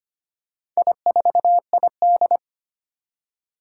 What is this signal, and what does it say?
Morse code, heard as a keyed tone, about 25 wpm, 700 Hz. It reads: I4ID
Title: Four Challenging Letter-Number Combinations 25wpm